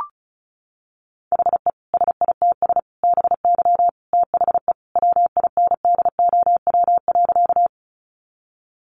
Morse code, heard as a keyed tone, about 34 wpm, 700 Hz. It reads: HE SITS BY THE WINDOW.
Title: Easy Phrases 35wpm